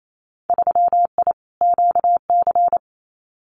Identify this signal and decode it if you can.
Morse code, heard as a keyed tone, about 28 wpm, 700 Hz. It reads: 3I QC
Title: Winter Field Day Exchanges Encore 28wpm